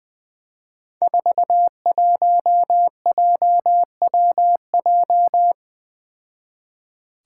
Morse code, heard as a keyed tone, about 20 wpm, 700 Hz. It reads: 41JWJ
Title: Five Challenging Letter-Number Combinations 20wpm